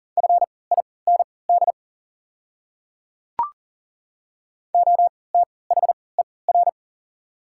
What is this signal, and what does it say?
Morse code, heard as a keyed tone, about 38 wpm, spaced as for 22 wpm, 690 Hz. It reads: FIND OTHER
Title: 3x Character Spacing - Top 100 Words - 40wpm